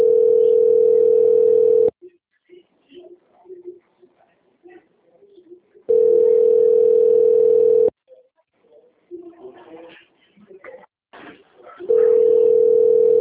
Ringback tone (North America), 440 + 480 Hz.